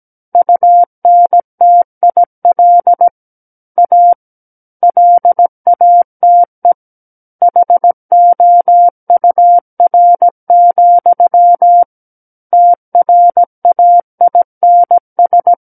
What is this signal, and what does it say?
Morse code, heard as a keyed tone, about 17 wpm, 700 Hz. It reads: UNTIL A LATE HOUR, TRAINS
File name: Book One / 08-Chapter 8 - 17 WPM